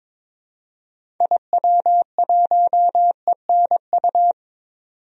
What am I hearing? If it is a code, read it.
Morse code, heard as a keyed tone, about 22 wpm, 700 Hz. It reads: IW1ENU
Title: DXCC-100 Call Signs - 6 Characters - 22wpm